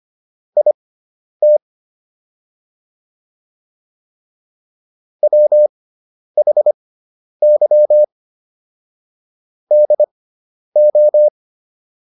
Morse code, heard as a keyed tone, about 25 wpm, spaced as for 10 wpm, 600 Hz. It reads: IT WHY DO